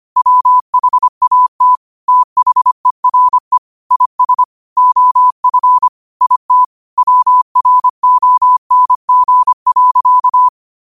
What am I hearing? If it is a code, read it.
Morse code, heard as a keyed tone, about 25 wpm, 1000 Hz. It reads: WHAT THERE IS OF IT WRONG.